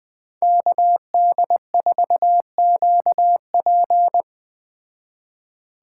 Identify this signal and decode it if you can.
Morse code, heard as a keyed tone, about 20 wpm, 700 Hz. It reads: KD4QP
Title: Speed Racing - USA 2x2 Call Signs 17wpm